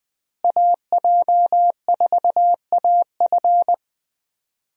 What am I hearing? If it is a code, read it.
Morse code, heard as a keyed tone, about 20 wpm, 700 Hz. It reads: AJ4AF
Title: USA Call Signs - 5 Characters - Repeated 3x - 20wpm